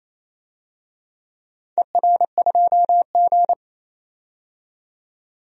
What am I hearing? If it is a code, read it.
Morse code, heard as a keyed tone, about 28 wpm, 700 Hz. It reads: ER2G
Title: DXCC-100 Call Signs - 4 Characters - 28wpm